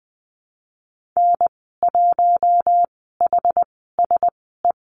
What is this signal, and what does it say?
Morse code, heard as a keyed tone, about 20 wpm, spaced as for 14 wpm, 700 Hz. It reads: N1HSE
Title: 062 Lesson 11g - Letter h - Call Signs 20wpm